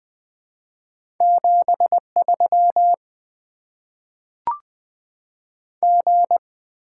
Morse code, heard as a keyed tone, about 20 wpm, 700 Hz. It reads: 73 G
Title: Top 20 QSO Elements - 6x Character Spacing - 20wpm